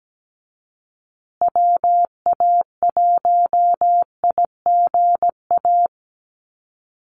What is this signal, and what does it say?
Morse code, heard as a keyed tone, about 17 wpm, 700 Hz. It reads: WA1IGA
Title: USA Call Signs Repeated 3x 17wpm